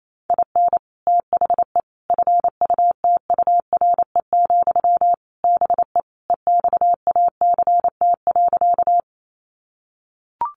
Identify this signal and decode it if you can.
Morse code, heard as a keyed tone, about 28 wpm, 700 Hz. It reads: IN THE FUTURE, BE EXACT.
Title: Sentences from Top 1300 Words 28wpm